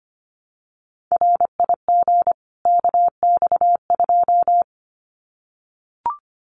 Morse code, heard as a keyed tone, about 25 wpm, 700 Hz. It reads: RIG KX2
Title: Common QSO Phrases 25wpm